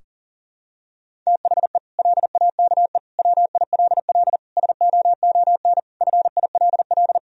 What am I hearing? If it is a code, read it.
Morse code, heard as a keyed tone, about 37 wpm, 700 Hz. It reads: THE LAKE WILL SOON FILL
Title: Sentences from Top 1000 Words - Review Part 3 - 40wpm